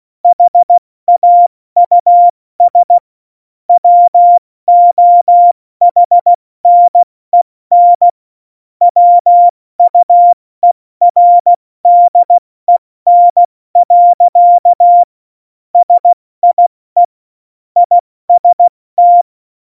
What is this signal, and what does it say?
Morse code, heard as a keyed tone, about 16 wpm, 700 Hz. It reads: HAUS WOHNEN WUERDEN. SIE IST
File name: Morsekurs / Witze / Witz 21. Januar 2020, 70 F80